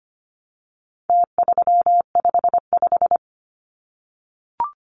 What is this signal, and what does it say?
Morse code, heard as a keyed tone, about 25 wpm, 700 Hz. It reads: T355
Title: Numbers - Contest Style - 4-Digit 25wpm